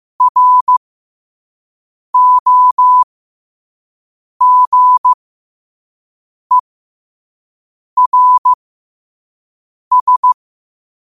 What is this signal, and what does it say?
Morse code, heard as a keyed tone, about 15 wpm, spaced as for 5 wpm, 1000 Hz. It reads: ROGERS
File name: qotdfiles / 5wpm